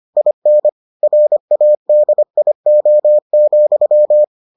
Morse code, heard as a keyed tone, about 25 wpm, 600 Hz. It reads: IN RADIO,